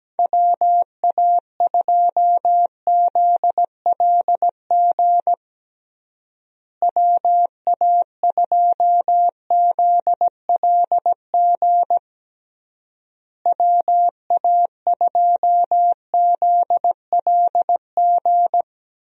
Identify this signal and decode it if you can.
Morse code, heard as a keyed tone, about 17 wpm, 700 Hz. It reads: WA2ZLG WA2ZLG WA2ZLG
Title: USA Call Signs - 6 Characters - Repeated 3x - 17wpm